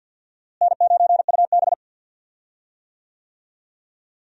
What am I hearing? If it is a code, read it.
Morse code, heard as a keyed tone, about 47 wpm, 700 Hz. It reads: N9UB
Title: USA Call Signs - 4 Characters - Repeated 3x - 50wpm